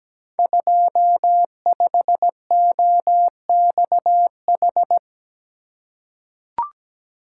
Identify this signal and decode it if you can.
Morse code, heard as a keyed tone, about 17 wpm, 700 Hz. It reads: 25OXH
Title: Code Groups - 5 Letter:Numbers - 17wpm